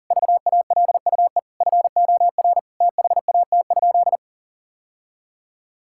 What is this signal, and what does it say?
Morse code, heard as a keyed tone, about 40 wpm, 700 Hz. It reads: VALUE FOR THAT?
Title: Sentences from Top 1000 Words - Review Part 6 - 40wpm